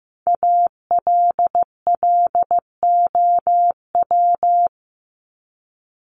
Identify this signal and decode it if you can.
Morse code, heard as a keyed tone, about 15 wpm, 700 Hz. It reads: ALLOW
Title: Top 501-750 Words 15wpm